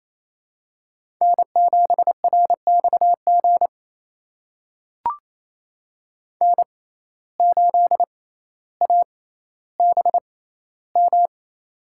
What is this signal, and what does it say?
Morse code, heard as a keyed tone, about 28 wpm, spaced as for 14 wpm, 700 Hz. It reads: N7RXG N8ABM
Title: USA Call Signs - 5 Characters - 6x Character Spacing - 28wpm